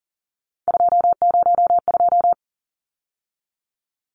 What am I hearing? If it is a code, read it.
Morse code, heard as a keyed tone, about 40 wpm, 700 Hz. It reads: 202
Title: Numbers - 3-Digit 40wpm